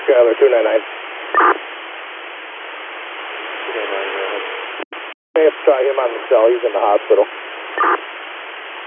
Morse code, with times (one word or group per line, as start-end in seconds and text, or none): none